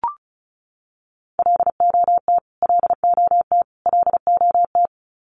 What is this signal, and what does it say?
Morse code, heard as a keyed tone, about 34 wpm, 700 Hz. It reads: LOT LOT LOT
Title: Top 500 Words Repeated 35wpm